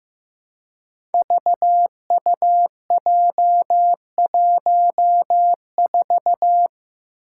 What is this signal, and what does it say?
Morse code, heard as a keyed tone, about 15 wpm, 700 Hz. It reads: VUJ14
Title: Five Challenging Letter-Number Combinations 15wpm